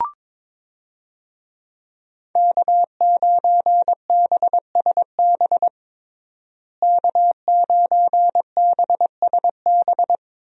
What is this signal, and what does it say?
Morse code, heard as a keyed tone, about 22 wpm, 700 Hz. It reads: K9BSB K9BSB
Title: USA Call Signs - 5 Characters - Repeated 3x - 22wpm